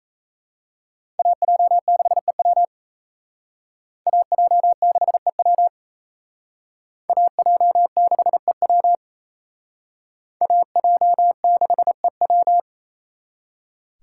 Morse code, heard as a keyed tone, about 37 wpm, 700 Hz. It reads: AJ6EW AJ6EW AJ6EW AJ6EW